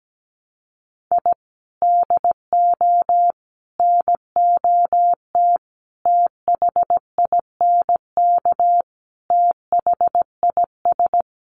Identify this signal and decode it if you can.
Morse code, heard as a keyed tone, about 17 wpm, 700 Hz. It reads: I DO NOT THINK THIS
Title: Sets of 5 Words Encore 17wpm